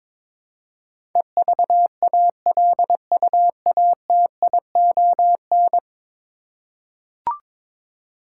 Morse code, heard as a keyed tone, about 22 wpm, 700 Hz. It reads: EVALUATION
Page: morse-code-ninja-practice-22wpm